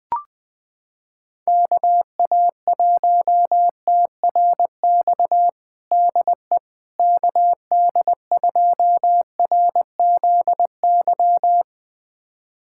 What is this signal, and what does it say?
Morse code, heard as a keyed tone, about 20 wpm, 700 Hz. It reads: KA1TRX DE KD2RZY
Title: USA Callsign DE Callsign - 6 Characters - 20wpm